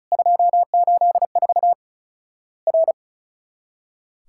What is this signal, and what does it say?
Morse code, heard as a keyed tone, about 34 wpm, 700 Hz. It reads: 284 R
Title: Worked All Europe DX Contest - QTC Exchange - 35wpm